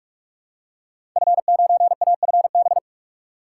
Morse code, heard as a keyed tone, about 46 wpm, 700 Hz. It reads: F9AFB
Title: Code Groups - 5 Letter:Numbers - 45wpm